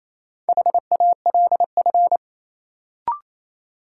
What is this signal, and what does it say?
Morse code, heard as a keyed tone, about 28 wpm, 700 Hz. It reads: HALF
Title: Sets of 3 Words - New Word Drill - 28wpm